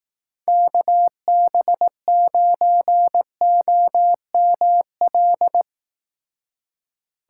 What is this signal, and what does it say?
Morse code, heard as a keyed tone, about 18 wpm, 700 Hz. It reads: KB9OML
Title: Speed Racing - USA 2x3 Call Signs 15wpm